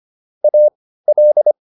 Morse code, heard as a keyed tone, about 25 wpm, spaced as for 15 wpm, 600 Hz. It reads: AL